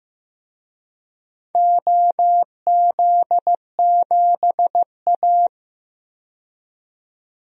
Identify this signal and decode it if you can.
Morse code, heard as a keyed tone, about 15 wpm, 700 Hz. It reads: OZ7A
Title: DXCC-100 Call Signs - 4 Characters - 15wpm